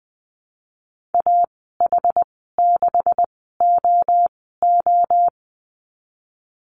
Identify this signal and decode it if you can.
Morse code, heard as a keyed tone, about 20 wpm, spaced as for 15 wpm, 700 Hz. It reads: AH6OO
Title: 147 Lesson 24g - Letter y - Call Signs 20wpm